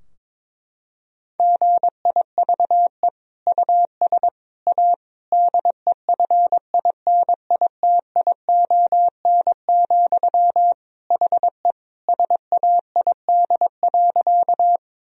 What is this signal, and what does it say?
Morse code, heard as a keyed tone, about 22 wpm, 700 Hz. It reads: GIVE US A DEFINITION, HE SAID.
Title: Sentences from Top 1000 Words - Review Part 4 - 22wpm